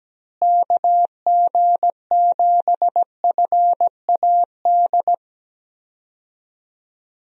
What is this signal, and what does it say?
Morse code, heard as a keyed tone, about 17 wpm, 700 Hz. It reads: KG7FAD
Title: USA Call Signs - 6 Characters - Repeated 2x - 17wpm